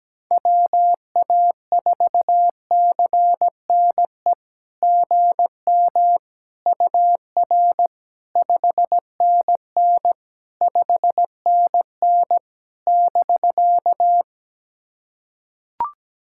Morse code, heard as a keyed tone, about 17 wpm, 700 Hz. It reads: WA4CNE GM UR 5NN 5NN <BK>
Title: SOTA QSO Practice 17wpm